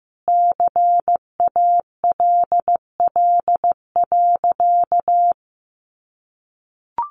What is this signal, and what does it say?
Morse code, heard as a keyed tone, about 14 wpm, 700 Hz. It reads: CALL.
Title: Sentences from Top 1300 Words 15wpm